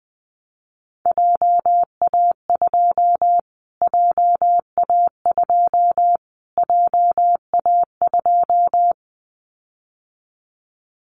Morse code, heard as a keyed tone, about 20 wpm, 700 Hz. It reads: JA2 JA2 JA2
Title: Callsign Prefixes for Top 25 DXCC - Repeated 3x - 20wpm